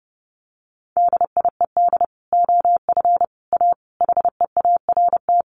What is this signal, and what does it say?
Morse code, heard as a keyed tone, about 30 wpm, 700 Hz. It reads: DIED OF A HEART
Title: Sets of 5 Words Encore2 30wpm